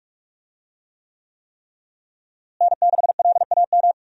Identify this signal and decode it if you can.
Morse code, heard as a keyed tone, about 46 wpm, 700 Hz. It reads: N6LAM
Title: USA Call Signs - 5 Characters - Repeated 3x - 45wpm